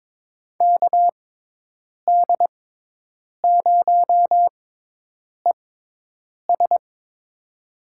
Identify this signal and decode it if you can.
Morse code, heard as a keyed tone, about 22 wpm, spaced as for 8 wpm, 700 Hz. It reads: KD0ES
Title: USA Call Signs - 5 Characters - 6x Character Spacing - 22wpm